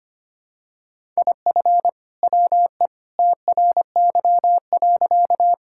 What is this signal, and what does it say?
Morse code, heard as a keyed tone, about 25 wpm, 700 Hz. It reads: IF WE TRY.
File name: practice / 25wpm / Sentences with 2-Word Phrases - 25wpm